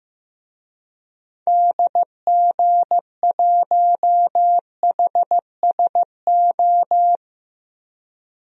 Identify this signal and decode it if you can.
Morse code, heard as a keyed tone, about 15 wpm, 700 Hz. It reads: DG1HSO
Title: DXCC-100 Call Signs - 6 Characters - Repeated 2x - 15wpm